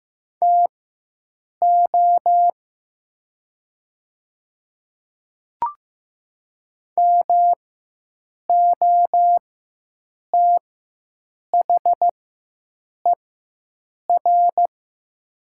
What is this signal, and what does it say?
Morse code, heard as a keyed tone, about 15 wpm, spaced as for 7 wpm, 700 Hz. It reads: TO MOTHER
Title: 4x Character Spacing - Top 300 Words - 15wpm